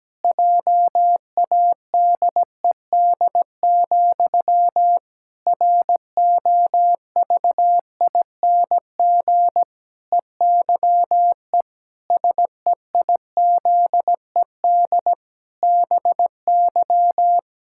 Morse code, heard as a keyed tone, about 17 wpm, 700 Hz. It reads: JADED, ROVING EYE SEIZED BY
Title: Pangram Challenge v2 - 17wpm